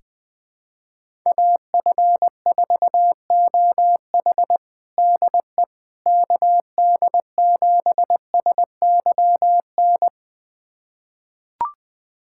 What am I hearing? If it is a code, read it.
Morse code, heard as a keyed tone, about 21 wpm, 700 Hz. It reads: AF4OH DE KD7SYN
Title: USA Callsign DE Callsign 20wpm